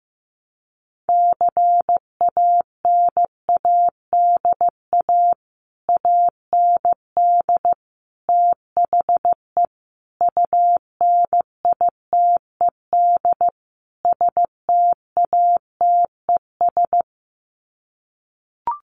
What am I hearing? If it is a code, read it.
Morse code, heard as a keyed tone, about 15 wpm, 700 Hz. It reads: CANADA AND THE UNITED STATES
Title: Sets of 5 Words Encore 15wpm